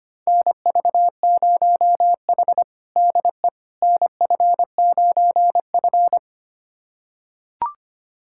Morse code, heard as a keyed tone, about 25 wpm, 700 Hz. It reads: NV0H DE NF9F
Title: USA Callsign DE Callsign - 4 Characters - 25wpm